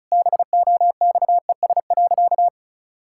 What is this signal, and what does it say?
Morse code, heard as a keyed tone, about 36 wpm, 700 Hz. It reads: BOXES.
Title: Pangram Challenge v2 - 35wpm